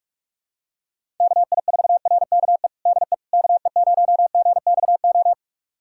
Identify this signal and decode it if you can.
Morse code, heard as a keyed tone, about 46 wpm, 700 Hz. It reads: KI4RKE DE KE0GXO
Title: USA Callsign DE Callsign - 6 Characters - 45wpm